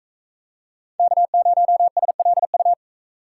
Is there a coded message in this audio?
Yes: Morse code from 1.0 to 2.7 s, heard as a keyed tone, about 41 wpm, 700 Hz. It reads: K0SLU